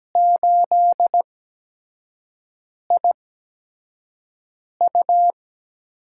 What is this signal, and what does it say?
Morse code, heard as a keyed tone, about 17 wpm, spaced as for 5 wpm, 700 Hz. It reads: 8IU
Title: USA Call Signs - 5 Characters - 8x Character Spacing - 17wpm